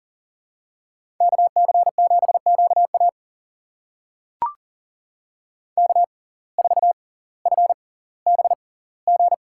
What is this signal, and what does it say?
Morse code, heard as a keyed tone, about 40 wpm, spaced as for 20 wpm, 700 Hz. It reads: KC7QA K4FBG
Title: USA Call Signs - 5 Characters - 6x Character Spacing - 40wpm